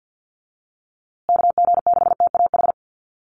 Morse code, heard as a keyed tone, about 50 wpm, 700 Hz. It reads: KG6TAH
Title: USA Call Signs Repeated 3x 50wpm